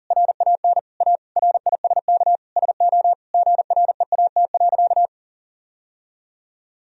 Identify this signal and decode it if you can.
Morse code, heard as a keyed tone, about 40 wpm, 700 Hz. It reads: RAN A RISK SO GREAT.
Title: Sentences from Top 1000 Words - Review Part 4 - 40wpm